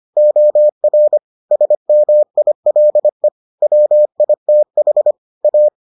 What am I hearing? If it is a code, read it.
Morse code, heard as a keyed tone, about 25 wpm, 600 Hz. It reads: OR SMILE WITH A